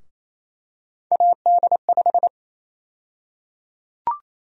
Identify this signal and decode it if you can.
Morse code, heard as a keyed tone, about 28 wpm, 700 Hz. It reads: AD5
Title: USA Call Sign Prefixes - Repeated 2x - 28wpm